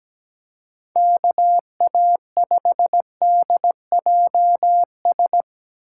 Morse code, heard as a keyed tone, about 20 wpm, 700 Hz. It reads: KA5DJS
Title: Speed Racing - USA 2x3 Call Signs 17wpm